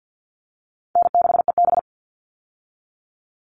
Morse code, heard as a keyed tone, about 50 wpm, 700 Hz. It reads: N6EB
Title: All Cut Numbers - 4-Digit 50wpm